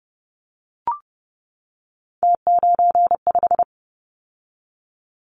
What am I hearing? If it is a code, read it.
Morse code, heard as a keyed tone, about 30 wpm, 700 Hz. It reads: T95
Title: Callsign Prefixes for Top 100 DXCC - 30wpm